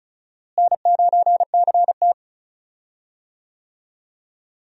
Morse code, heard as a keyed tone, about 34 wpm, 700 Hz. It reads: N9CT
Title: USA Call Signs - 4 Characters - Repeated 3x - 35wpm